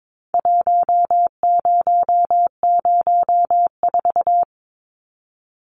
Morse code, heard as a keyed tone, about 22 wpm, 700 Hz. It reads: 1004
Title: Numbers - Contest Style - 4-Digit 22wpm